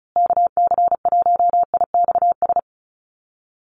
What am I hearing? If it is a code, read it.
Morse code, heard as a keyed tone, about 34 wpm, 700 Hz. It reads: KC1IXS
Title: USA Call Signs Repeated 3x 35wpm